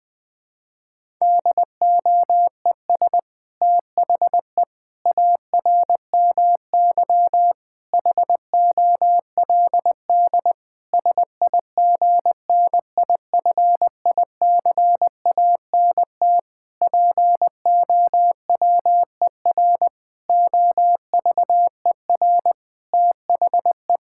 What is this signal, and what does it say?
Morse code, heard as a keyed tone, about 20 wpm, 700 Hz. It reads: DOES THE ARMY HOLD SIGNIFICANT POWER OVER THE
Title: Sentences from Top 1000 Words - Review Part 6 - 20wpm